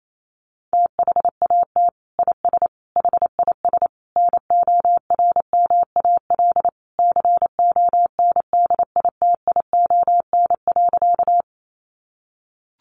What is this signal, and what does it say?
Morse code, heard as a keyed tone, about 28 wpm, 700 Hz. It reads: THAT IS HIS NORMAL CONDITION.
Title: Sentences from Top 1200 Words 28wpm